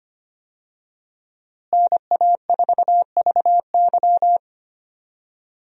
Morse code, heard as a keyed tone, about 25 wpm, 700 Hz. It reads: NA4VY
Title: USA Call Signs - 5 Characters - Repeated 2x - 25wpm